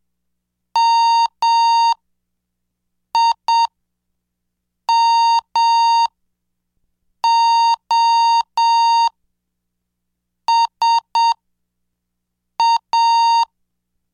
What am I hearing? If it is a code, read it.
Morse code, heard as a keyed tone, about 7 wpm, spaced as for 5 wpm, 920 Hz. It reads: MIMOSA